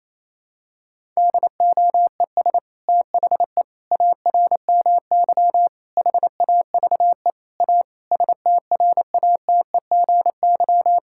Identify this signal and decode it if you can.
Morse code, heard as a keyed tone, about 28 wpm, 700 Hz. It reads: DOES THE ARMY HAVE A STRATEGY
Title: Sentences from Top 1000 Words - Review Part 5 - 28wpm